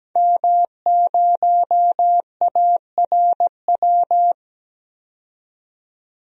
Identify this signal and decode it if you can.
Morse code, heard as a keyed tone, about 17 wpm, 700 Hz. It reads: M0ARW
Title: DXCC-100 Call Signs - 5 Characters - 17wpm